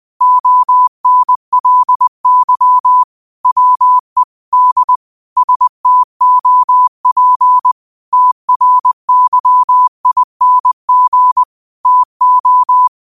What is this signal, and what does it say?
Morse code, heard as a keyed tone, about 20 wpm, 1000 Hz. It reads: ONLY WE D STOP TRYING TO